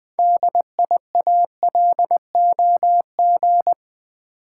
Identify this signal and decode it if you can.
Morse code, heard as a keyed tone, about 20 wpm, 700 Hz. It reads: DIALOG
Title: Build A Word - 2 Letters At A Time - 8-Letter Words - 20wpm